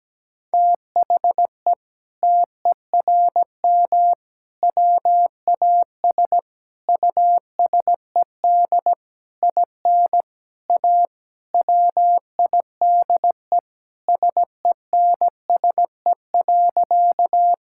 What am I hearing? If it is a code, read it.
Morse code, heard as a keyed tone, about 17 wpm, 700 Hz. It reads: THE TERM WAS USED IN A WIDE SENSE.
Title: Sentences from Top 1000 Words - Review Part 6 - 17wpm